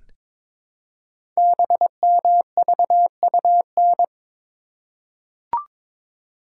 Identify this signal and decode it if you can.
Morse code, heard as a keyed tone, about 22 wpm, 700 Hz. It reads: BMVUN
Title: Code Groups - 5 Letters - 22wpm